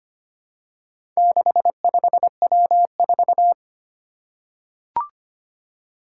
Morse code, heard as a keyed tone, about 25 wpm, 700 Hz. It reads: 65W4
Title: Four Challenging Letter-Number Combinations 25wpm